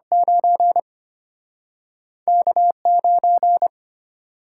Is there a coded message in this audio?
Yes: Morse code from 0.1 to 3.7 s, heard as a keyed tone, about 33 wpm, 700 Hz. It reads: 9 K9